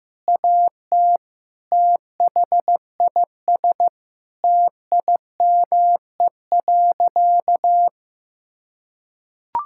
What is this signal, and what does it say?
Morse code, heard as a keyed tone, about 15 wpm, 700 Hz. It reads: AT THIS TIME.
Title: Sentences from Top 1000 Words - Review Part 5 - 15wpm